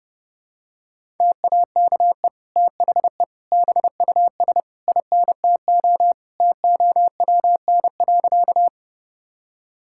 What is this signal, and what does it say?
Morse code, heard as a keyed tone, about 32 wpm, 700 Hz. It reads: TAKE THE BUS INTO TOWN.
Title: Sentences from Top 1000 Words - Review Part 6 - 30wpm